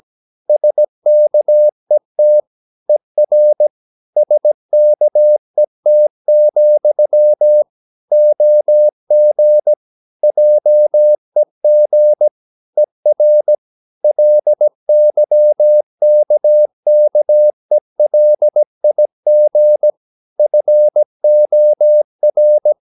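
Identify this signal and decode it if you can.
Morse code, heard as a keyed tone, about 17 wpm, 600 Hz. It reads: SKET ER SKET, OG JEG ER LYKKELIG FOR
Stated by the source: Slow (QRS) Morse practice at 17 wpm, from CW med Gnister.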